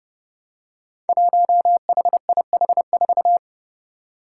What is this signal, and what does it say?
Morse code, heard as a keyed tone, about 32 wpm, 700 Hz. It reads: 1HIH4
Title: Five Challenging Letter-Number Combinations 30wpm